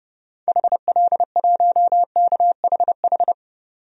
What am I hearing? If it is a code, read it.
Morse code, heard as a keyed tone, about 29 wpm, 690 Hz. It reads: HL1KHH
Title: DXCC-100 Call Signs - 6 Characters - 30wpm